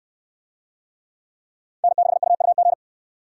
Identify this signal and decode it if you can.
Morse code, heard as a keyed tone, about 66 wpm, 700 Hz. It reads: N6UUG